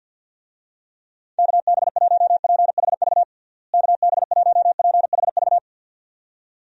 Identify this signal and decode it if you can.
Morse code, heard as a keyed tone, about 50 wpm, 700 Hz. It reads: KB1PHV KB1PHV
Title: USA Call Signs - 6 Characters - Repeated 2x - 50wpm